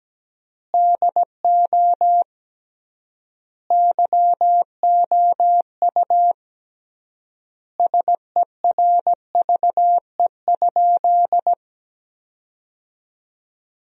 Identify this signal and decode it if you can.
Morse code, heard as a keyed tone, about 17 wpm, 700 Hz. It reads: DO YOU SERVE?
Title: 3x Word-Spacing - Sentences from Top 400 Words - 17wpm